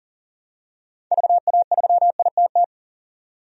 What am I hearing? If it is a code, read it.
Morse code, heard as a keyed tone, about 37 wpm, 700 Hz. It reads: VA3ITT
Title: DXCC-100 Call Signs - 6 Characters - 40wpm